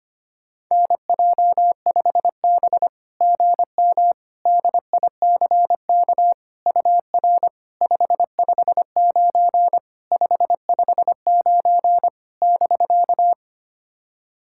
Morse code, heard as a keyed tone, about 25 wpm, 690 Hz. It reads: NJ5B GM DICK UR 559 559 <BK>
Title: SOTA QSO Practice 25wpm